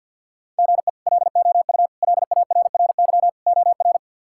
Morse code, heard as a keyed tone, about 51 wpm, 700 Hz. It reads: GE LOU LARRY OR
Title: K1USN SST QSO Practice - 50wpm